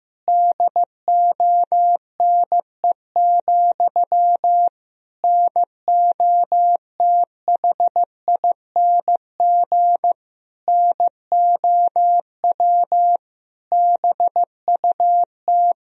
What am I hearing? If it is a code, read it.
Morse code, heard as a keyed tone, about 15 wpm, 700 Hz. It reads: DONE, NOTHING NOW BUT
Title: Sentences from Top 1000 Words - Review Part 4 - 15wpm